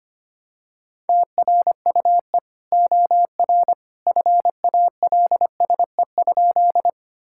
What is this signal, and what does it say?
Morse code, heard as a keyed tone, about 25 wpm, 700 Hz. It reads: TRUE OR FALSE?
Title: Sentences from Top 1600 Words 25wpm